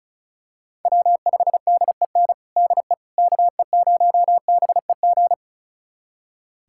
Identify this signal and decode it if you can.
Morse code, heard as a keyed tone, about 36 wpm, 700 Hz. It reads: W5DEN DE KE0BEG
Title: USA Callsign DE Callsign 35wpm